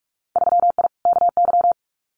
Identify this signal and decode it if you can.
Morse code, heard as a keyed tone, about 46 wpm, 700 Hz. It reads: 3I KY
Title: Winter Field Day Exchanges 45wpm